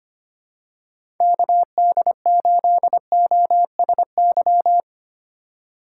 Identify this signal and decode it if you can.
Morse code, heard as a keyed tone, about 25 wpm, 700 Hz. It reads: KD8OSY
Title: USA Call Signs - 6 Characters - 4x Character Spacing - 25wpm